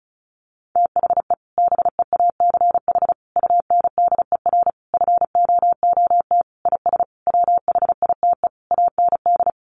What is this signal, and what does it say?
Morse code, heard as a keyed tone, about 34 wpm, 700 Hz. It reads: THE BEACH UNDER FOOT IS WHITE AND
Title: Sentences from Top 1500 Words 35wpm